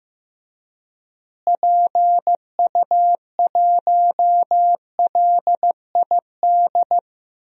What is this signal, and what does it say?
Morse code, heard as a keyed tone, about 15 wpm, 700 Hz. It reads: PU1LID
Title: DXCC-100 Call Signs - 6 Characters - Repeated 3x - 15wpm